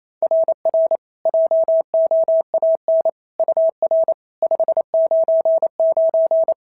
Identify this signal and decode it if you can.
Morse code, heard as a keyed tone, about 25 wpm, 650 Hz. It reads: RR JOAN UR 599
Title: SOTA QSO Practice 28wpm